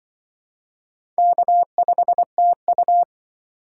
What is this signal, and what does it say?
Morse code, heard as a keyed tone, about 24 wpm, 700 Hz. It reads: K5TU
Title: Speed Racing - USA 1x2 Call Signs 20wpm